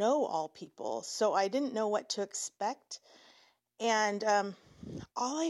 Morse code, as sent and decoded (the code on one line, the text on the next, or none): none